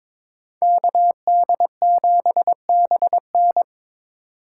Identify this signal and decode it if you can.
Morse code, heard as a keyed tone, about 26 wpm, 700 Hz. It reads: KD7BN
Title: Speed Racing - USA 2x2 Call Signs 22wpm